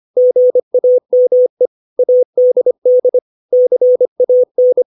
Morse code, heard as a keyed tone, about 25 wpm, 500 Hz. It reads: GAME ADD CAN
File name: 632 words / 632 Words 25WPM